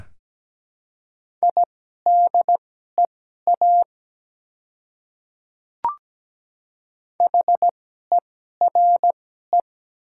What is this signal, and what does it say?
Morse code, heard as a keyed tone, about 17 wpm, spaced as for 12 wpm, 700 Hz. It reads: IDEA HERE
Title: 2x Character Spacing - Top 300 Words - 17wpm